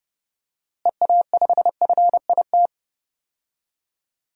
Morse code, heard as a keyed tone, about 30 wpm, 700 Hz. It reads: EA5FIT
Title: DXCC-100 Call Signs - 6 Characters - Repeated 3x - 30wpm